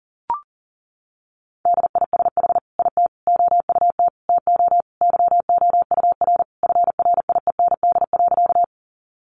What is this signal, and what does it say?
Morse code, heard as a keyed tone, about 37 wpm, 700 Hz. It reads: DISH IT OUT TO YOUR FRIEND.
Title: Sentences from Top 1300 Words 40wpm